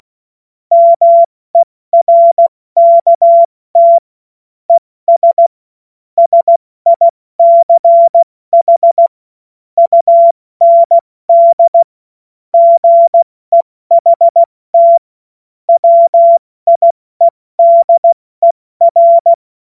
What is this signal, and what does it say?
Morse code, heard as a keyed tone, about 16 wpm, 680 Hz. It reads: MERKT ES SICH UND GEHT WIEDER
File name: Morsekurs / Witze / Witz 24. August 2021, 70 F80